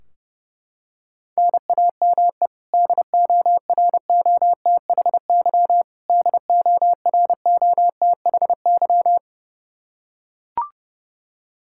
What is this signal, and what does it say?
Morse code, heard as a keyed tone, about 30 wpm, 700 Hz. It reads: NAME DOROTHY DOROTHY
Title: Practice QSO - Exchange 1 - Beginning - 30wpm